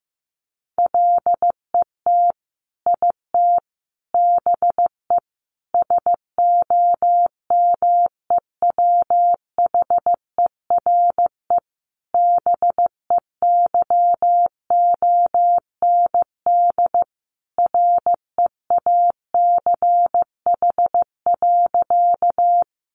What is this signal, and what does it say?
Morse code, heard as a keyed tone, about 15 wpm, 700 Hz. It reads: LET IT BE SOMEWHERE BEYOND REACH.
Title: Sentences from Top 1400 Words 15wpm